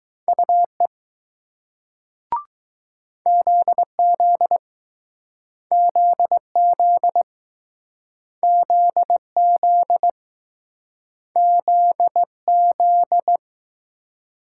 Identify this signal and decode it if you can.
Morse code, heard as a keyed tone, about 20 wpm, 700 Hz. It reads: UE ZZ ZZ ZZ ZZ